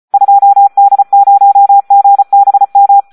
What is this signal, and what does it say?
Morse code, heard as a keyed tone, about 34 wpm, 800 Hz. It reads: 2D0GBM
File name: Audio practica CW / SD 1 34